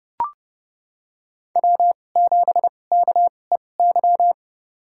Morse code, heard as a keyed tone, about 31 wpm, spaced as for 22 wpm, 700 Hz. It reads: W7KEY
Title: USA Call Signs - 5 Characters - 2x Character Spacing - 30wpm